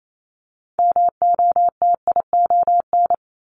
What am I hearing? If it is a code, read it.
Morse code, heard as a keyed tone, about 28 wpm, 700 Hz. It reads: MOTION
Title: Multi-letter Phonemes 28wpm